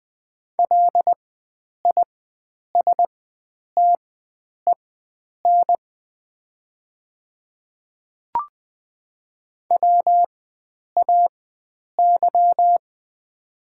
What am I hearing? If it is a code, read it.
Morse code, heard as a keyed tone, about 20 wpm, spaced as for 9 wpm, 700 Hz. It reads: LISTEN WAY
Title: 4x Character Spacing - Top 500 Words - 20wpm